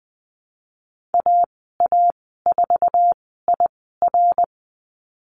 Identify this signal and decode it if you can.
Morse code, heard as a keyed tone, about 20 wpm, spaced as for 15 wpm, 700 Hz. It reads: AA4IR
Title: 055 Lesson 10g - Letter r - Call Signs 20wpm